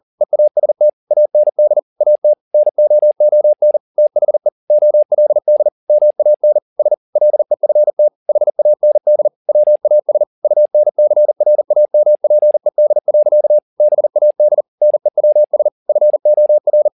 Morse code, heard as a keyed tone, about 40 wpm, 600 Hz. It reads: EAST AND AT NOON THE OLD MAN S LEFT HAND WAS UNCRAMPED. BAD NEWS FOR